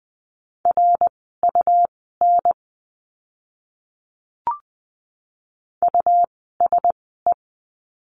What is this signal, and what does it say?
Morse code, heard as a keyed tone, about 20 wpm, spaced as for 15 wpm, 700 Hz. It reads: RUN USE
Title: 101 Lesson 17f - Letter u - Words 20wpm